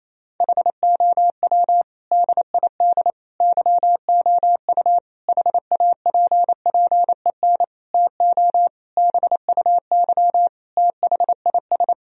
Morse code, heard as a keyed tone, about 28 wpm, 700 Hz. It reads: HOW DID YOU HAPPEN TO BUY THIS
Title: Sentences from Top 1000 Words - Review Part 4 - 28wpm